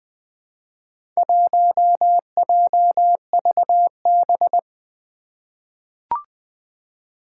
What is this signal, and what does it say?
Morse code, heard as a keyed tone, about 20 wpm, 700 Hz. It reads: 1JVB
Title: Four Challenging Letter-Number Combinations 20wpm